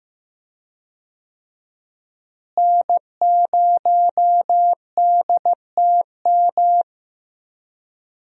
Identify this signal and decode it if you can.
Morse code, heard as a keyed tone, about 15 wpm, 700 Hz. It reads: N0DTM
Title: USA Call Signs - 5 Characters - Repeated 3x - 15wpm